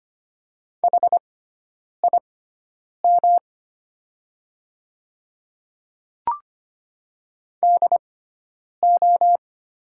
Morse code, heard as a keyed tone, about 25 wpm, spaced as for 9 wpm, 690 Hz. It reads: HIM DO
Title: 6x Character Spacing - Top 200 Words - 25wpm